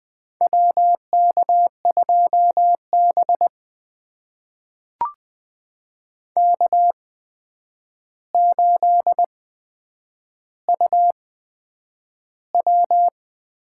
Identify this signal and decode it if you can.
Morse code, heard as a keyed tone, about 20 wpm, spaced as for 9 wpm, 700 Hz. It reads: WK2B K8UW
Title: USA Call Signs - 4 Characters - 8x Character Spacing - 20wpm